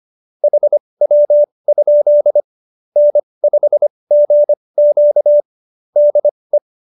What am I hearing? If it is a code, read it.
Morse code, heard as a keyed tone, about 25 wpm, spaced as for 20 wpm, 590 Hz. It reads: HW? N5GQ DE